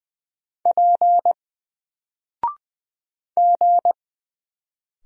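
Morse code, heard as a keyed tone, about 20 wpm, 700 Hz. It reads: P G
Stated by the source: Single letter and number Morse drill at 20 wpm, from N4LCD.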